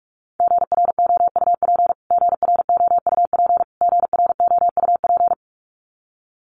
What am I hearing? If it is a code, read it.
Morse code, heard as a keyed tone, about 44 wpm, 700 Hz. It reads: GROUP GROUP GROUP
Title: Top 500 Words Repeated 45wpm